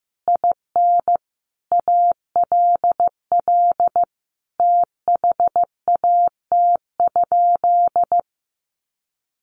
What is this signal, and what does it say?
Morse code, heard as a keyed tone, about 15 wpm, 700 Hz. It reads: IN ALL THAT?
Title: Sentences from Top 1500 Words 15wpm